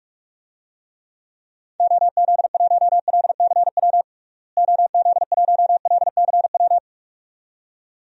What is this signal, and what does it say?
Morse code, heard as a keyed tone, about 41 wpm, 700 Hz. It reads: OZ1LCW OZ1LCW
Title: DXCC-100 Call Signs - 6 Characters - Repeated 2x - 45wpm